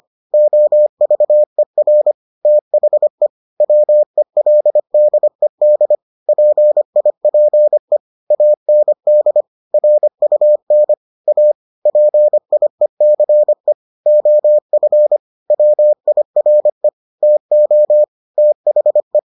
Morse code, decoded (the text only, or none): OVER THE WELDED PIPE AND RUN A PIECE OF WIRE TO THE